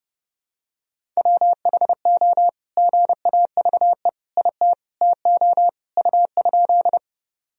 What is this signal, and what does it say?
Morse code, heard as a keyed tone, about 32 wpm, 700 Hz. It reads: WHO GAVE IT TO U?
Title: Sentences from Top 500 Words and Abbreviations 30wpm